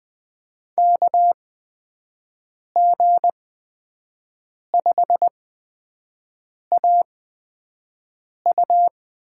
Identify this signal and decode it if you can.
Morse code, heard as a keyed tone, about 20 wpm, spaced as for 5 wpm, 700 Hz. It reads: KG5AU
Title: USA Call Signs - 5 Characters - 8x Character Spacing - 20wpm